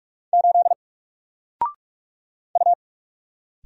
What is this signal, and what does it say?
Morse code, heard as a keyed tone, about 44 wpm, 700 Hz. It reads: 8 U
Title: Single Character (Letter:Number:Punctuation:Prosign) 45wpm